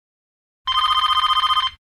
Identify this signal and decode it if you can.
There is no code.